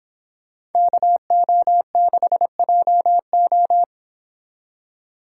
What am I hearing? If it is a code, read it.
Morse code, heard as a keyed tone, about 29 wpm, 700 Hz. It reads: KO6JO